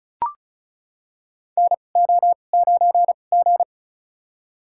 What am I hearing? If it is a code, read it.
Morse code, heard as a keyed tone, about 36 wpm, spaced as for 25 wpm, 700 Hz. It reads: NO9G
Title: USA Call Signs - 4 Characters - 2x Character Spacing - 35wpm